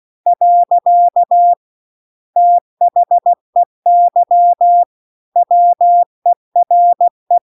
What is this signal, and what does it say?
Morse code, heard as a keyed tone, about 16 wpm, 700 Hz. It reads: . THEY WERE